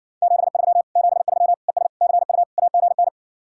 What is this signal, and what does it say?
Morse code, heard as a keyed tone, about 67 wpm, 700 Hz. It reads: 73 73 ES CU AGN